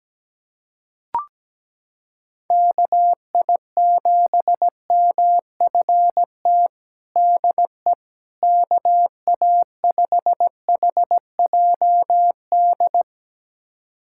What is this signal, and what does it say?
Morse code, heard as a keyed tone, about 17 wpm, 700 Hz. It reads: KI7MFT DE KA5HJD
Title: USA Callsign DE Callsign - 6 Characters - 17wpm